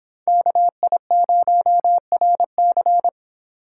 Morse code, heard as a keyed tone, about 25 wpm, 700 Hz. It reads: KI0RC